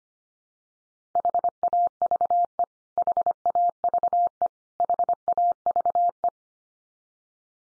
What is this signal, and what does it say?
Morse code, heard as a keyed tone, about 25 wpm, 700 Hz. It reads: HAVE HAVE HAVE
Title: Top 100 Words 25wpm x3